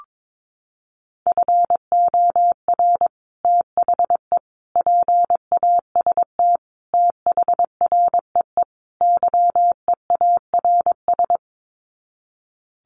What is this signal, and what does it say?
Morse code, heard as a keyed tone, about 22 wpm, 700 Hz. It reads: FOR THE PAST THREE YEARS
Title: Sets of 5 Words Encore2 22wpm